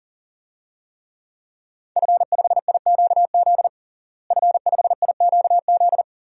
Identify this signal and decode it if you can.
Morse code, heard as a keyed tone, about 40 wpm, 690 Hz. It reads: F5IQZ F5IQZ
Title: DXCC-100 Call Signs - 5 Characters - Repeated 2x - 40wpm